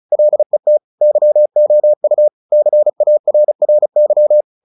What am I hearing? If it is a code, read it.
Morse code, heard as a keyed tone, about 35 wpm, 600 Hz. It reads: LET YOU CARRY
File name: the old man and the sea / the old man and the sea 1 35wpm